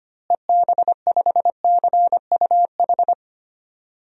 Morse code, heard as a keyed tone, about 25 wpm, 700 Hz. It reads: EB5CUH
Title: DXCC-100 Call Signs - 6 Characters - Repeated 3x - 25wpm